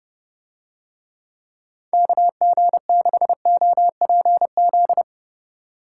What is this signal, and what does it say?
Morse code, heard as a keyed tone, about 28 wpm, 700 Hz. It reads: KG6OPZ